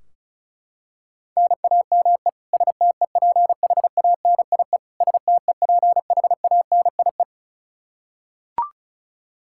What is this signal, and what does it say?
Morse code, heard as a keyed tone, about 36 wpm, 700 Hz. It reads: NAME STEPHANIE STEPHANIE
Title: Practice QSO - Exchange 1 - Beginning - 35wpm